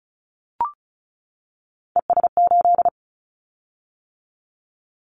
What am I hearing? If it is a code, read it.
Morse code, heard as a keyed tone, about 36 wpm, 700 Hz. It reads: ES8
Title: Callsign Prefixes for Top 100 DXCC - 35wpm